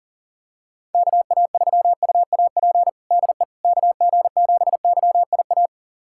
Morse code, heard as a keyed tone, about 40 wpm, 700 Hz. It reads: KA3UAP DE KG7YIA
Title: USA Callsign DE Callsign - 6 Characters - 40wpm